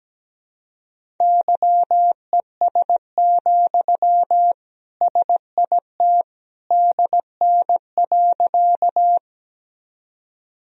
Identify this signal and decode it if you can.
Morse code, heard as a keyed tone, about 17 wpm, 700 Hz. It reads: YES, SIT DN.
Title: Sentences from Top 500 Words and Abbreviations 17wpm